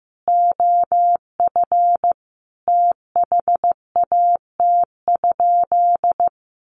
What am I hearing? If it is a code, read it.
Morse code, heard as a keyed tone, about 15 wpm, 700 Hz. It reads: OF THAT?
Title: Sentences from Top 1200 Words 15wpm